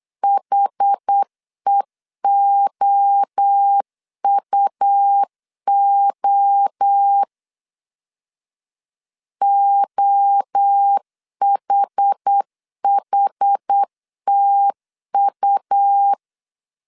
Morse code, heard as a keyed tone, about 8 wpm, 800 Hz. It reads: HEOUO OHHTU
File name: CW / DET / 1 3 1500C S B 8wpm 4141